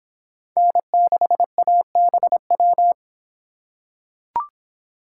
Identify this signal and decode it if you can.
Morse code, heard as a keyed tone, about 25 wpm, 700 Hz. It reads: N6ABW